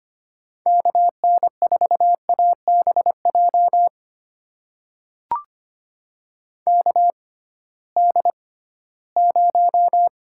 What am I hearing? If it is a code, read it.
Morse code, heard as a keyed tone, about 25 wpm, spaced as for 16 wpm, 700 Hz. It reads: KN4ABJ KD0
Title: USA Call Signs - 6 Characters - 6x Character Spacing - 25wpm